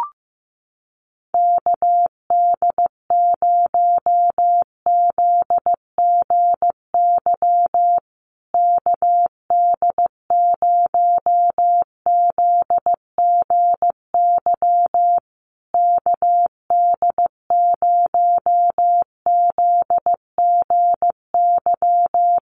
Morse code, heard as a keyed tone, about 15 wpm, 700 Hz. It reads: KD0ZGY KD0ZGY KD0ZGY
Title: USA Call Signs Repeated 3x 15wpm